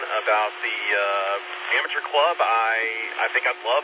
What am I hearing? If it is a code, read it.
Emergency Alert System (SAME) tones, heard as harsh buzzing bursts, with speech. It is heard from 0.6 to 3.1 s.